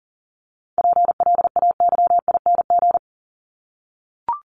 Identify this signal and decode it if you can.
Morse code, heard as a keyed tone, about 40 wpm, 700 Hz. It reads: PLAYING
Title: ING Suffix - Top 100 Words with Base Words 40wpm